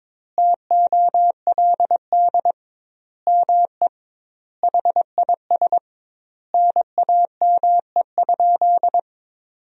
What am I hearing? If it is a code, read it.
Morse code, heard as a keyed tone, about 22 wpm, 700 Hz. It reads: TOLD ME HIS NAME?
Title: 2x Word-Spacing - Sentences from Top 400 Words - 22wpm